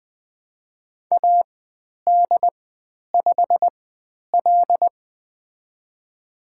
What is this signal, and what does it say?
Morse code, heard as a keyed tone, about 20 wpm, spaced as for 10 wpm, 700 Hz. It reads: AD5L
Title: 102 Lesson 17g - Letter u - Call Signs 20wpm